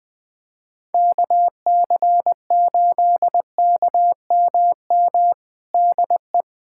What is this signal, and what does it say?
Morse code, heard as a keyed tone, about 20 wpm, 700 Hz. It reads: KC8KMM DE
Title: USA Callsign DE Callsign 20wpm